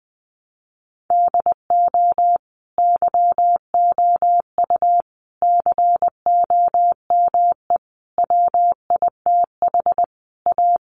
Morse code, heard as a keyed tone, about 20 wpm, 700 Hz. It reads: DO YOU COME WITH A
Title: Sentences from Top 1200 Words 20wpm